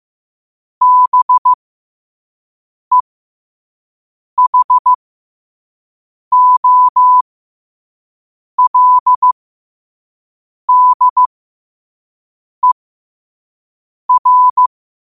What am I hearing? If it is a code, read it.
Morse code, heard as a keyed tone, about 15 wpm, spaced as for 5 wpm, 1000 Hz. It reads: BEHOLDER
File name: qotdfiles / 5wpm